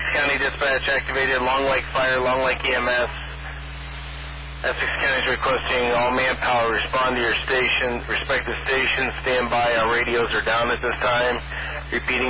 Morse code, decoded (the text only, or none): none